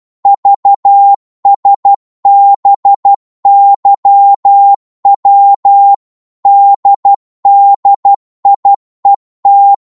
Morse code, heard as a keyed tone, about 12 wpm, 800 Hz. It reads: VSBYW DDIET
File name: Morse / ltr-12wpm